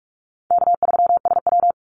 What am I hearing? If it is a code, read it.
Morse code, heard as a keyed tone, about 44 wpm, 700 Hz. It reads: K3SW
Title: Easy Call Signs - 45wpm